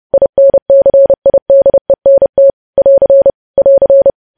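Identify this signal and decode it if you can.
Morse code, heard as a keyed tone, about 30 wpm, 550 Hz. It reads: INCIDENT <AR> <AR>
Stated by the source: Morse news headlines at 30 wpm.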